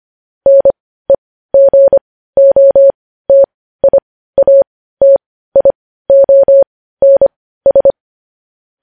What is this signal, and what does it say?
Morse code, heard as a keyed tone, about 25 wpm, spaced as for 15 wpm, 550 Hz. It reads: NEGOTIATIONS
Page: news-headlines-in-morse-code-at-15-wpm